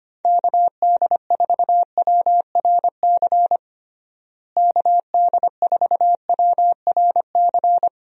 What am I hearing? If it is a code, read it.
Morse code, heard as a keyed tone, about 25 wpm, 700 Hz. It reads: KD4WRC KD4WRC
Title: USA Call Signs - 6 Characters - Repeated 3x - 25wpm